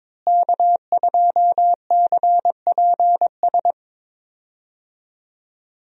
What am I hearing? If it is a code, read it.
Morse code, heard as a keyed tone, about 22 wpm, 700 Hz. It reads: K2CPS
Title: USA Call Signs - 5 Characters - Repeated 3x - 22wpm